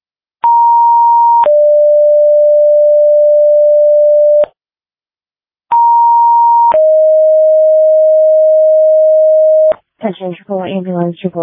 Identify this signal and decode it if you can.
Two-tone page, heard as a one-second tone, then a three-second tone, about 940 then 590 Hz, about 940 then 620 Hz, with speech. It is heard from 0.4 to 9.7 s.